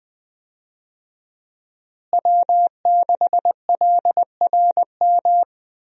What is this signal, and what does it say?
Morse code, heard as a keyed tone, about 21 wpm, 700 Hz. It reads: W6LRM
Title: USA Call Signs - 5 Characters - Repeated 3x - 20wpm